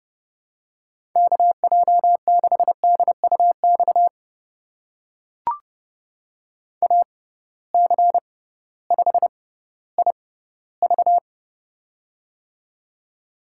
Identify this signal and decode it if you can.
Morse code, heard as a keyed tone, about 30 wpm, spaced as for 16 wpm, 700 Hz. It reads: KJ6DUX AC5IV
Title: USA Call Signs - 6x Character Spacing - 30wpm